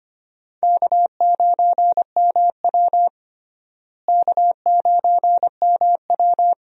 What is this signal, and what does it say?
Morse code, heard as a keyed tone, about 25 wpm, 700 Hz. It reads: K9MW K9MW
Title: USA Call Signs - 4 Characters - Repeated 2x - 25wpm